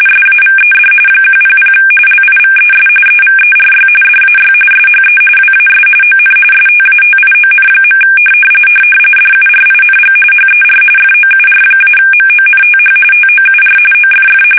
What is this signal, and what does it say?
RTTY, about 45.45 baud. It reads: SAVED 100 LIVES EVERY DAY SHE WAS ON STATION. DURING HER FOUR-MONTH TOUR OFF
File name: fsk850 / battleshipnj